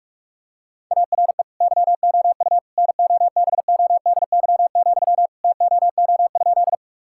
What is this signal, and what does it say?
Morse code, heard as a keyed tone, about 46 wpm, 700 Hz. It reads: ARE YOU NOBODY, TOO?
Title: Sentences from Top 1600 Words 45wpm